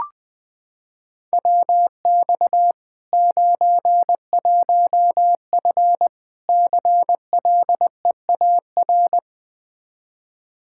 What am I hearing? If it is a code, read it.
Morse code, heard as a keyed tone, about 20 wpm, 700 Hz. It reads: WX 91F CLEAR
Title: K1USN SST QSO Practice 20wpm